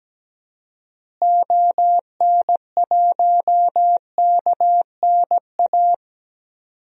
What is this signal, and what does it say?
Morse code, heard as a keyed tone, about 17 wpm, 700 Hz. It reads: ON1KNA
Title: DXCC-100 Call Signs - 6 Characters - Repeated 3x - 17wpm